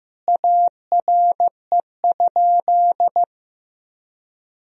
Morse code, heard as a keyed tone, about 15 wpm, 700 Hz. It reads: ARE?
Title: Sentences from Top 1000 Words - Review Part 4 - 15wpm